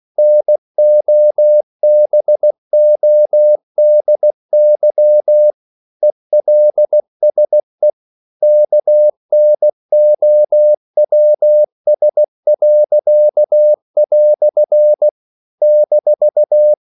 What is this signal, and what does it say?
Morse code, heard as a keyed tone, about 16 wpm, 600 Hz. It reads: NOBODY ELSE KNOWS." -
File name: CW / Practice / EnglishQuotations0216